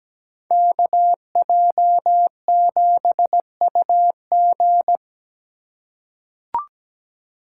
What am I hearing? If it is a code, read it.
Morse code, heard as a keyed tone, about 17 wpm, 700 Hz. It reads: KJ7UG
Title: USA Call Signs - 2x Character Spacing - 17wpm